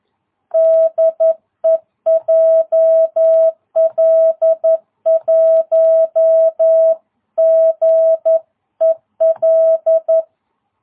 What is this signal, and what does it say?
Morse code, heard as a keyed tone, about 11 wpm, 650 Hz. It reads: DE JL1GEL
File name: jh8cbh / 2022cw